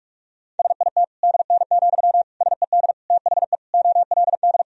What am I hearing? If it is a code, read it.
Morse code, heard as a keyed tone, about 41 wpm, 700 Hz. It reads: SIT DN, SED THE OLD
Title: Sentences from Top 500 Words and Abbreviations 45wpm